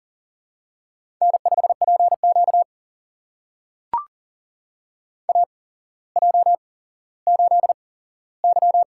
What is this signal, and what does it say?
Morse code, heard as a keyed tone, about 42 wpm, spaced as for 17 wpm, 700 Hz. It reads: N5PQ AJ8Y
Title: USA Call Signs - 4 Characters - 8x Character Spacing - 40wpm